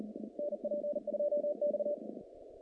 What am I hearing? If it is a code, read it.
Morse code, heard as a keyed tone, about 38 wpm, 590 Hz, with noise. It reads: NP2X
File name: contests / naqp-cw-k5tr-op-k5pi- -2100